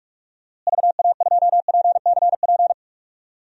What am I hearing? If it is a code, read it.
Morse code, heard as a keyed tone, about 45 wpm, 700 Hz. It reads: VA2PCP
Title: DXCC-100 Call Signs - 6 Characters - Repeated 3x - 45wpm